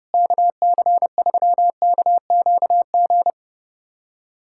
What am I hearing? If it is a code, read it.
Morse code, heard as a keyed tone, about 30 wpm, 700 Hz. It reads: KC3KQG
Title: USA Callsign DE Callsign - 6 Characters - 30wpm